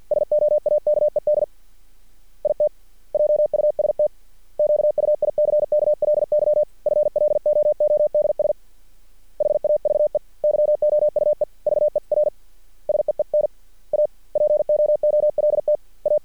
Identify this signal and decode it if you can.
Morse code, heard as a keyed tone, about 50 wpm, 600 Hz. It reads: SOAKED IT JUST QUICKLY FLOODS HAVE YOUE VER SEEN A POOLTA
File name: archived / CRSnet